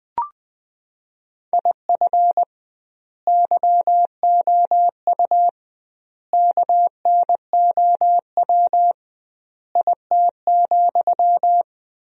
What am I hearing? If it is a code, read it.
Morse code, heard as a keyed tone, about 20 wpm, 700 Hz. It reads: IF YOU KNOW IT,
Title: 2x Word-Spacing - Sentences from Top 400 Words - 20wpm